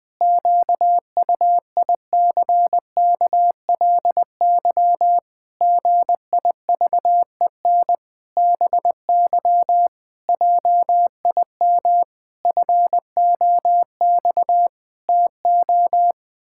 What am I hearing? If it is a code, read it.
Morse code, heard as a keyed tone, about 20 wpm, 700 Hz. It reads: QUICKLY GIVEN BY JIM FOX TO
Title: Pangram Challenge v2 - 20wpm